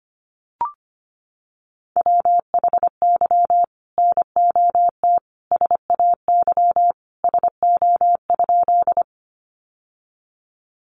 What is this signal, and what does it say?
Morse code, heard as a keyed tone, about 25 wpm, 700 Hz. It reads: WHY NOT SAY SO?
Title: Sentences from Top 600 Words 25wpm